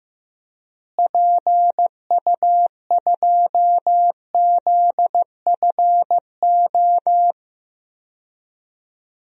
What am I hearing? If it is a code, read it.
Morse code, heard as a keyed tone, about 15 wpm, 700 Hz. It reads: PU2ZFO
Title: DXCC-100 Call Signs - 6 Characters - Repeated 2x - 15wpm